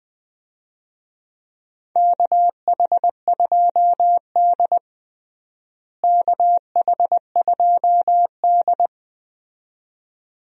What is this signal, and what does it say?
Morse code, heard as a keyed tone, about 20 wpm, 700 Hz. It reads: KH2D KH2D
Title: USA Call Signs - 4 Characters - Repeated 2x - 20wpm